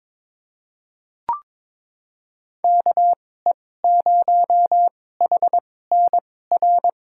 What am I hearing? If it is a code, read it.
Morse code, heard as a keyed tone, about 22 wpm, spaced as for 16 wpm, 700 Hz. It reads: KE0HNR
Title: USA Call Signs - 6 Characters - 2x Character Spacing - 22wpm